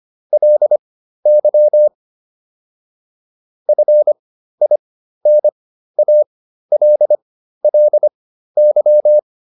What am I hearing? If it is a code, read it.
Morse code, heard as a keyed tone, about 25 wpm, spaced as for 13 wpm, 600 Hz. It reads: LY FINALLY
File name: LY7 13